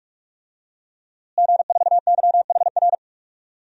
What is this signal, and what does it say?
Morse code, heard as a keyed tone, about 44 wpm, 700 Hz. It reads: G4YHR
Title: DXCC-100 Call Signs - 5 Characters - Repeated 3x - 45wpm